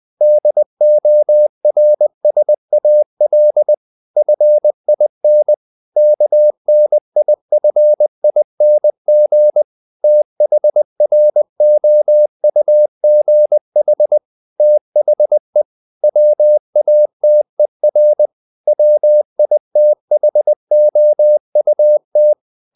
Morse code, heard as a keyed tone, about 20 wpm, 600 Hz. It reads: DORSAL FIN KNIFING THROUGH THE WATER WITHOUT
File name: the old man and the sea part 2 20wpm